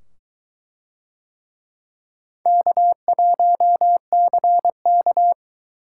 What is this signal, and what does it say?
Morse code, heard as a keyed tone, about 22 wpm, 700 Hz. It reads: K1CK